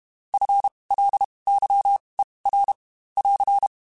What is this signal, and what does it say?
Morse code, heard as a keyed tone, about 32 wpm, spaced as for 23 wpm, 800 Hz. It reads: FLYER <AR>
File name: Spaced / FL Runs sp mp3s / FL Runs sp @30